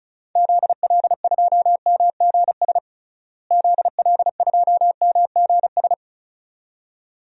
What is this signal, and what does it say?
Morse code, heard as a keyed tone, about 36 wpm, 700 Hz. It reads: ZL2MGS ZL2MGS
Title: DXCC-100 Call Signs - 6 Characters - Repeated 2x - 35wpm